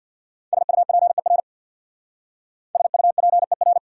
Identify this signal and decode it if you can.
Morse code, heard as a keyed tone, about 59 wpm, 700 Hz. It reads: SUPER SUPER
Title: Speed-Racing Prefixes and Suffixes 50wpm